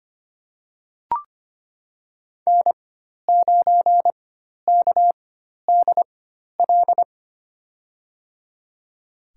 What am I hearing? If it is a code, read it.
Morse code, heard as a keyed tone, about 25 wpm, spaced as for 12 wpm, 700 Hz. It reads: N9KDL
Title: USA Call Signs - 5 Characters - 4x Character Spacing - 25wpm